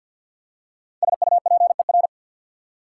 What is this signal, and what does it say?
Morse code, heard as a keyed tone, about 54 wpm, 690 Hz. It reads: SUPER
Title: Prefixes and Suffixes 50wpm